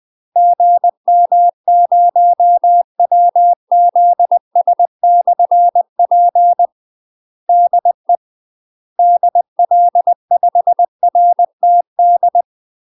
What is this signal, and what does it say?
Morse code, heard as a keyed tone, about 20 wpm, 700 Hz. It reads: GM0WZS/P DE DL5RTD
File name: UK MASTER QSOs TRAINING / 20WPM / UK MASTER QSOs 0045